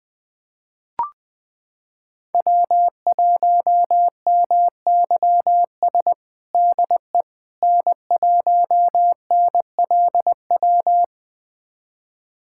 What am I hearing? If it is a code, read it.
Morse code, heard as a keyed tone, about 20 wpm, 700 Hz. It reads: W1MYS DE N1NLW
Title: USA Callsign DE Callsign - 5 Characters - 20wpm